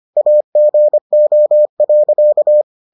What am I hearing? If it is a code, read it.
Morse code, heard as a keyed tone, about 25 wpm, 600 Hz. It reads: AGO.